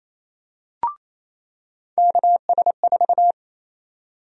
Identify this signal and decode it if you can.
Morse code, heard as a keyed tone, about 28 wpm, 700 Hz. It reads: KS4
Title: USA Call Sign Prefixes - 28wpm